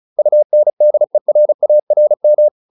Morse code, heard as a keyed tone, about 34 wpm, 600 Hz. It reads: UNDERARM
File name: UN8 35